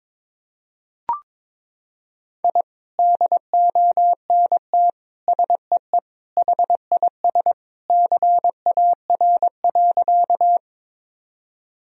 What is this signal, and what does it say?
Morse code, heard as a keyed tone, about 22 wpm, 700 Hz. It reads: I DONT SEE HIS CAR.
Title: Sentences from Top 1000 Words - Review Part 1 - 22wpm